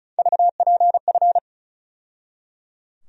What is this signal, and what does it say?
Morse code, heard as a keyed tone, about 34 wpm, 700 Hz. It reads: VPF
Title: USA Call Sign Postfixes - 3 Letters 35wpm